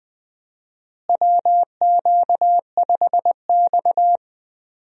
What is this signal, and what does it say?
Morse code, heard as a keyed tone, about 20 wpm, 700 Hz. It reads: WQ5X
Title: USA Call Signs - 4 Characters - Repeated 2x - 20wpm